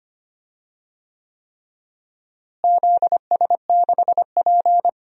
Morse code, heard as a keyed tone, about 25 wpm, 700 Hz. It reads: ZS6P
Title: DXCC-100 Call Signs - 4 Characters - Repeated 3x - 25wpm